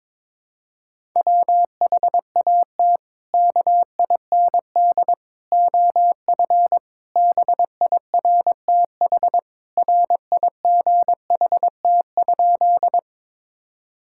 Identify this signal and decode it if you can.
Morse code, heard as a keyed tone, about 22 wpm, 700 Hz. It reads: WHAT KIND OF BIRTH RIGHT?
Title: Sentences from Top 1000 Words - Review Part 5 - 22wpm